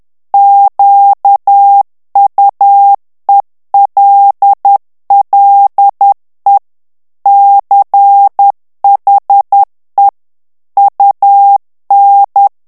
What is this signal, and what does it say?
Morse code, heard as a keyed tone, about 10 wpm, 800 Hz. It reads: QUELLE CHE UN
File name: morse mp3 / ITALIEN texte clair 1 vitesse 10 mots minute